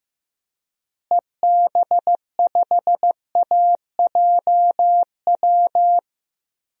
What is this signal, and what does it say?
Morse code, heard as a keyed tone, about 15 wpm, 700 Hz. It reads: EB5AJW
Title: DXCC-100 Call Signs - 6 Characters - Repeated 3x - 15wpm